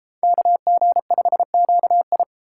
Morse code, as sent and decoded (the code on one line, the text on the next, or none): -.- --. ..... --.- ..
KG5QI